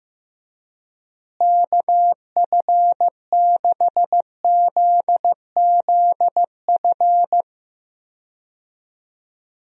Morse code, heard as a keyed tone, about 16 wpm, 700 Hz. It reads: KF6ZZF
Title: USA Call Signs - 6 Characters - Repeated 3x - 15wpm